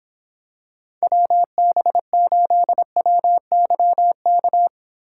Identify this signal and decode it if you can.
Morse code, heard as a keyed tone, about 26 wpm, 700 Hz. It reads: WB8WYK